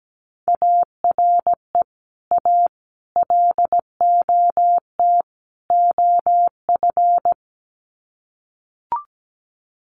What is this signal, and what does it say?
Morse code, heard as a keyed tone, about 17 wpm, 700 Hz. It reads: ARE A LOT OF
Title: Sets of 4 Words 17wpm